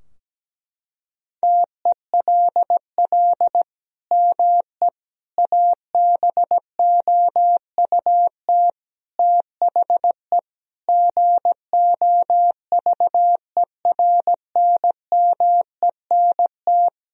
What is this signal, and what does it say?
Morse code, heard as a keyed tone, about 17 wpm, 700 Hz. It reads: TELL ME ABOUT THE GOVERNMENT
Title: Sentences from Top 1000 Words - Review Part 3 - 17wpm